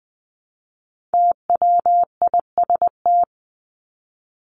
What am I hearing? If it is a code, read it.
Morse code, heard as a keyed tone, about 20 wpm, 700 Hz. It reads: TWIST
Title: Top 1751-2000 Words 20wpm